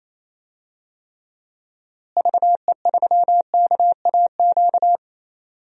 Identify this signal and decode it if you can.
Morse code, heard as a keyed tone, about 28 wpm, 700 Hz. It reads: VE3KAQ
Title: DXCC-100 Call Signs - 6 Characters - Repeated 2x - 28wpm